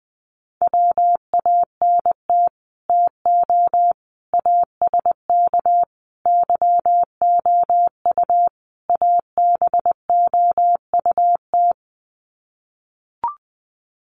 Morse code, heard as a keyed tone, about 20 wpm, 700 Hz. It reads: WANT TO ASK YOU ABOUT
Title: Sets of 5 Words Encore2 20wpm